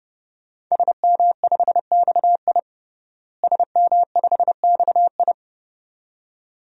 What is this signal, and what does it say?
Morse code, heard as a keyed tone, about 30 wpm, 700 Hz. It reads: SM5XI SM5XI
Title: DXCC-100 Call Signs - 5 Characters - Repeated 2x - 30wpm